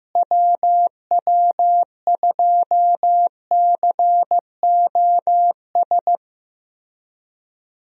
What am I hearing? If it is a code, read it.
Morse code, heard as a keyed tone, about 15 wpm, 700 Hz. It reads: WW2COS
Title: USA Call Signs - 6 Characters - Repeated 3x - 15wpm